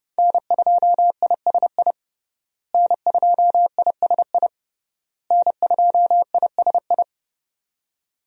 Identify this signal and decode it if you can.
Morse code, heard as a keyed tone, about 30 wpm, 700 Hz. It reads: N2ISI N2ISI N2ISI
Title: USA Call Signs - 5 Characters - Repeated 3x - 30wpm